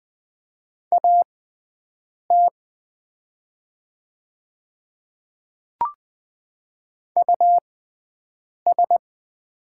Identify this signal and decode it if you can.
Morse code, heard as a keyed tone, about 20 wpm, spaced as for 7 wpm, 700 Hz. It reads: AT US
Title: 6x Character Spacing - Top 300 Words - 20wpm